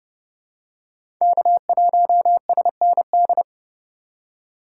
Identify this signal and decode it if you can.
Morse code, heard as a keyed tone, about 30 wpm, 700 Hz. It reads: K1SND
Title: K1USN SST QSO Practice - 30wpm